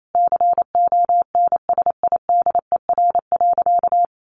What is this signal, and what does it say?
Morse code, heard as a keyed tone, about 28 wpm, 700 Hz. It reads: CONSIDER.
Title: Sentences from Top 900 Words 28wpm